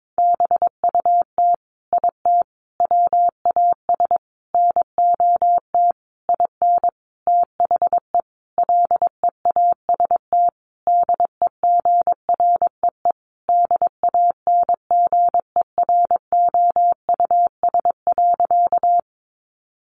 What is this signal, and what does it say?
Morse code, heard as a keyed tone, about 22 wpm, 700 Hz. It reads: BUT IT WAS NOT IN THE LEAST DEGREE DANGEROUS.
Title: Sentences from Top 1500 Words 22wpm